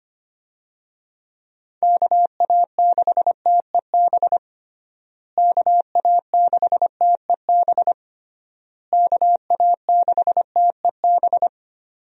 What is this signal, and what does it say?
Morse code, heard as a keyed tone, about 25 wpm, 700 Hz. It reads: KA6TEB KA6TEB KA6TEB
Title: USA Call Signs - 6 Characters - Repeated 3x - 25wpm